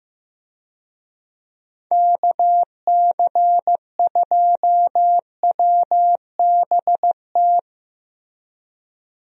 Morse code, heard as a keyed tone, about 16 wpm, 700 Hz. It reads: KC2WBT
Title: USA Call Signs - 6 Characters - Repeated 3x - 15wpm